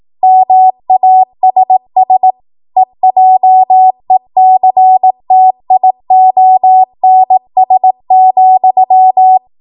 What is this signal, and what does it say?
Morse code, heard as a keyed tone, about 18 wpm, 750 Hz. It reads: MASS EJECTIONS,
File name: Morse / 18WPM